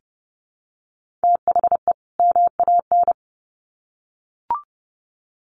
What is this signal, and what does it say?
Morse code, heard as a keyed tone, about 30 wpm, 700 Hz. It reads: THE MAN
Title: Sets of 2 Words 30wpm